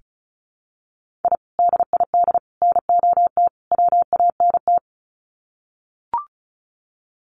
Morse code, heard as a keyed tone, about 34 wpm, 700 Hz. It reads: I DID NOT WANT
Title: Sets of 4 Words 35wpm